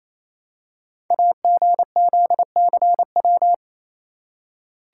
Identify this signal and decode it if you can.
Morse code, heard as a keyed tone, about 29 wpm, 700 Hz. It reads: AGZCW
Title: Code Groups - 5 Letters - 28wpm